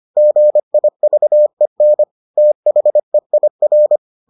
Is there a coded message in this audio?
Yes: Morse code from 0.2 to 4.0 s, heard as a keyed tone, about 25 wpm, 600 Hz. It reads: GIVEN THEIR